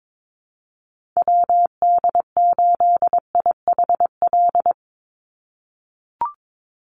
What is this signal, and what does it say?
Morse code, heard as a keyed tone, about 22 wpm, 700 Hz. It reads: WD8IHL
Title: USA Call Signs 22wpm